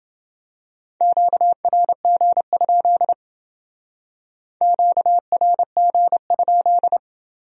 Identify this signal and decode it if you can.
Morse code, heard as a keyed tone, about 29 wpm, 700 Hz. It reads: QRG? QRG?